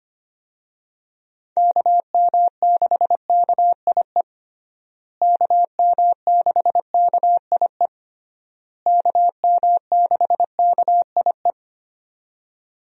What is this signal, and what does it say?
Morse code, heard as a keyed tone, about 25 wpm, 700 Hz. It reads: KM6KIE KM6KIE KM6KIE
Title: USA Call Signs - 6 Characters - Repeated 3x - 25wpm